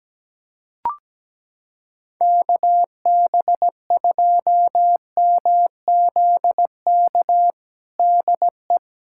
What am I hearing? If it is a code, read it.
Morse code, heard as a keyed tone, about 17 wpm, 700 Hz. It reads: KB2MZK DE
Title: USA Callsign DE Callsign 17wpm